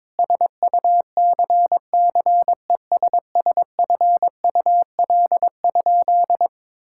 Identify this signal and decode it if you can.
Morse code, heard as a keyed tone, about 22 wpm, 700 Hz. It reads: SUCCESSFUL?
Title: Sentences from Top 1000 Words - Review Part 5 - 22wpm